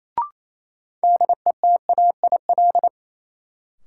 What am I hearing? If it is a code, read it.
Morse code, heard as a keyed tone, about 28 wpm, 700 Hz. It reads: DETAIL
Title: Sentences from Top 1500 Words - New Word Drill - 28wpm